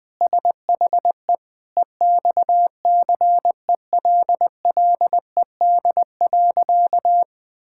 Morse code, heard as a keyed tone, about 20 wpm, 700 Hz. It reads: SHE EXCELLED.
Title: Sentences with 6-Word Phrases - 20wpm